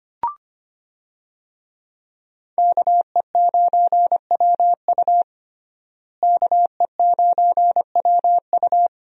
Morse code, heard as a keyed tone, about 25 wpm, 700 Hz. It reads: KE9WU KE9WU
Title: USA Call Signs - 5 Characters - Repeated 2x - 25wpm